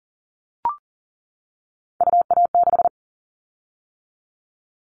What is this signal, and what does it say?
Morse code, heard as a keyed tone, about 37 wpm, 700 Hz. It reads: UA6
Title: Callsign Prefixes for Top 25 DXCC - 40wpm